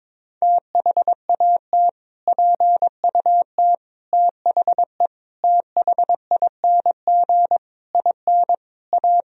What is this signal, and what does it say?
Morse code, heard as a keyed tone, about 22 wpm, 700 Hz. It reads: THAT PUT THE THING IN A
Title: Sentences from Top 1000 Words - Review Part 1 - 22wpm